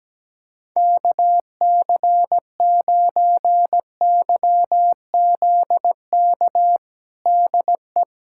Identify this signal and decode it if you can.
Morse code, heard as a keyed tone, about 17 wpm, 700 Hz. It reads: KC9YZK DE
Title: USA Callsign DE Callsign 17wpm